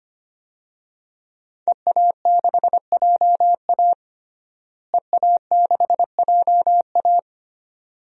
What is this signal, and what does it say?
Morse code, heard as a keyed tone, about 25 wpm, 700 Hz. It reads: EA6JA EA6JA
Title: DXCC-100 Call Signs - 5 Characters - Repeated 2x - 25wpm